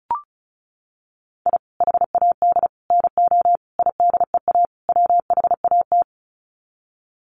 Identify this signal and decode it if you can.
Morse code, heard as a keyed tone, about 34 wpm, 700 Hz. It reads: I HAD NO IDEA WHAT
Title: Sets of 5 Words Encore 35wpm